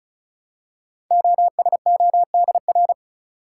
Morse code, heard as a keyed tone, about 36 wpm, 700 Hz. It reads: OSODR
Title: Code Groups - 5 Letters - 35wpm